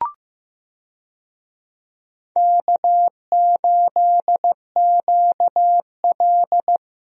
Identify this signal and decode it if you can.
Morse code, heard as a keyed tone, about 15 wpm, 700 Hz. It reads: K8QL
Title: USA Call Signs - 4 Characters - Repeated 3x - 15wpm